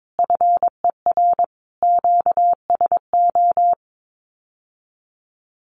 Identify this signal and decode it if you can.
Morse code, heard as a keyed tone, about 22 wpm, 700 Hz. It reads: FER QSO
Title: Common QSO Phrases Encore 22wpm